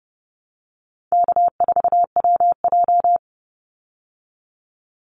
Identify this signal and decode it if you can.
Morse code, heard as a keyed tone, about 30 wpm, 700 Hz. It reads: K4WJ
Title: Easy Call Signs Repeated 3x - 30wpm